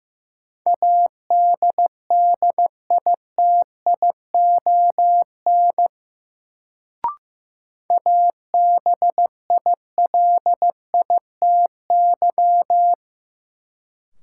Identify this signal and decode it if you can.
Morse code, heard as a keyed tone, about 15 wpm, 700 Hz. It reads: ADDITION ABILITY
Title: Sets of 3 Words - New Word Drill - 15wpm